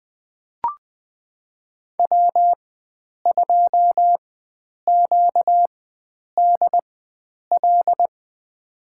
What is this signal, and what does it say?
Morse code, heard as a keyed tone, about 21 wpm, spaced as for 9 wpm, 700 Hz. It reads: W2QDL
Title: USA Call Signs - 5 Characters - 4x Character Spacing - 20wpm